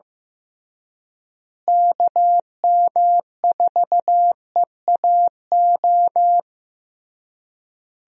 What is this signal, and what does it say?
Morse code, heard as a keyed tone, about 15 wpm, 700 Hz. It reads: KM4EAO
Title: USA Call Signs - 6 Characters - Repeated 3x - 15wpm